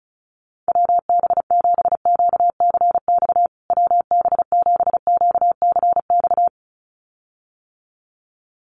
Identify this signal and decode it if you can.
Morse code, heard as a keyed tone, about 34 wpm, 700 Hz. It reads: WB7QCX WB7QCX
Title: USA Call Signs Repeated 2x 35wpm